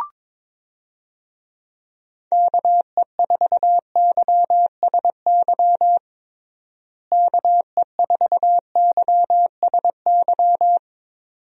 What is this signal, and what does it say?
Morse code, heard as a keyed tone, about 22 wpm, 700 Hz. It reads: KE4YSY KE4YSY
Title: USA Call Signs - 6 Characters - Repeated 2x - 22wpm